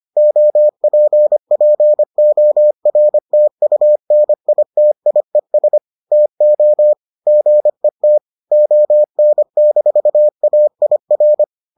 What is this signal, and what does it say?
Morse code, heard as a keyed tone, about 25 wpm, 600 Hz. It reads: OPPORTUNITIES TO GET ON-AIR